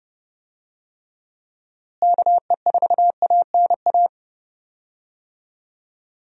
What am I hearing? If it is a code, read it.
Morse code, heard as a keyed tone, about 32 wpm, 700 Hz. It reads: KE4ANA
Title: USA Call Signs - 6 Characters - Repeated 3x - 30wpm